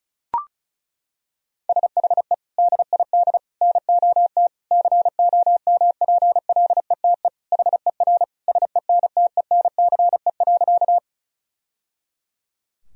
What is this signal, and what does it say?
Morse code, heard as a keyed tone, about 34 wpm, 700 Hz. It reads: SHE DID NOT COMPLETE HER SENTENCE.
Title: Sentences from Top 1000 Words - Review Part 2 - 35wpm